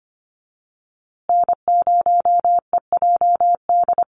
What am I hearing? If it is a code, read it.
Morse code, heard as a keyed tone, about 25 wpm, 700 Hz. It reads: N0EJD
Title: USA Call Signs Repeated 2x 25wpm